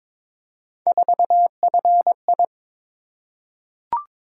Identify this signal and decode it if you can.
Morse code, heard as a keyed tone, about 22 wpm, 700 Hz. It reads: 4FI
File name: practice / 22wpm / Three Letter-Number Combinations - 22wpm